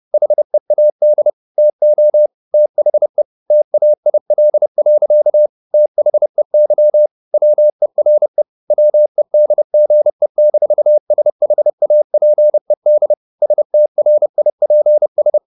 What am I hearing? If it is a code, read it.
Morse code, heard as a keyed tone, about 30 wpm, 600 Hz. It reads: HEAD TO THE TAIL. THEY WERE WEDGE-SHAPED STRIPS
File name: the old man and the sea / the old man and the sea 1 30wpm